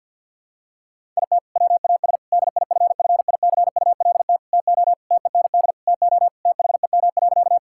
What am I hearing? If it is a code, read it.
Morse code, heard as a keyed tone, about 51 wpm, 700 Hz. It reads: IT WAS DIFFICULT TO TEND TO THEM.
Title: Sentences from Top 1000 Words - Review Part 5 - 50wpm